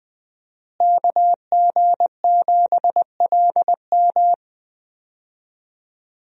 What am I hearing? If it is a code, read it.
Morse code, heard as a keyed tone, about 20 wpm, 700 Hz. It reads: KG7LM
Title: USA Call Signs - 5 Characters - Repeated 2x - 20wpm